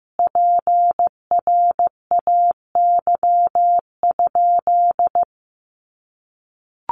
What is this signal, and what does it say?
Morse code, heard as a keyed tone, about 15 wpm, 700 Hz. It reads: PRAY?
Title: A Sound Of Thunder 15wpm